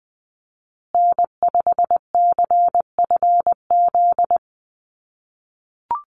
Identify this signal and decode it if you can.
Morse code, heard as a keyed tone, about 20 wpm, 700 Hz. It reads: N5CFZ
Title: USA Call Signs Repeated 2x 20wpm